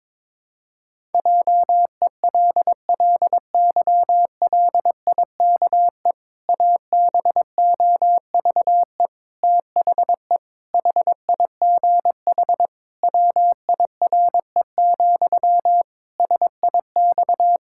Morse code, heard as a keyed tone, about 22 wpm, 700 Hz. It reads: JELLYLIKE ABOVE THE HIGH WIRE, SIX
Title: Pangram Challenge v2 - 22wpm